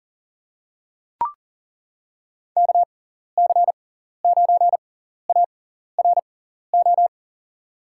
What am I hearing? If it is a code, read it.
Morse code, heard as a keyed tone, about 40 wpm, spaced as for 14 wpm, 700 Hz. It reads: KC9ARO
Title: USA Call Signs - 6 Characters - 6x Character Spacing - 40wpm